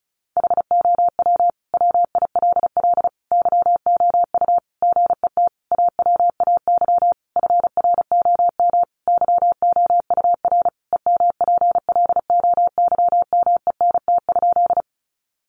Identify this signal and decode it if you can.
Morse code, heard as a keyed tone, about 34 wpm, 700 Hz. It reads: HOW WILL YOU GET AWAY FROM YOUR EMPLOYMENT?
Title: Sentences from Top 1500 Words 35wpm